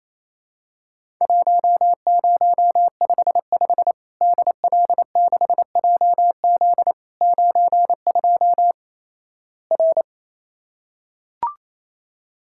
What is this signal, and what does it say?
Morse code, heard as a keyed tone, about 28 wpm, 700 Hz. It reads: 1055 DL6JZ 92 R
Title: Worked All Europe DX Contest - QTC Exchange - 28wpm